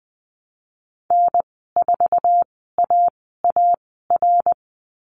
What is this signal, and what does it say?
Morse code, heard as a keyed tone, about 20 wpm, spaced as for 14 wpm, 700 Hz. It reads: N4AAR
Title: 062 Lesson 11g - Letter h - Call Signs 20wpm